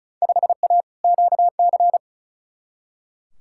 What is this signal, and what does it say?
Morse code, heard as a keyed tone, about 36 wpm, 700 Hz. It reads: 5A QC
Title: Field Day Exchanges Encore - 35wpm